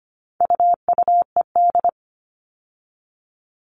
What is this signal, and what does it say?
Morse code, heard as a keyed tone, about 25 wpm, 700 Hz. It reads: UUED
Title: All Cut Numbers - 4-Digit 25wpm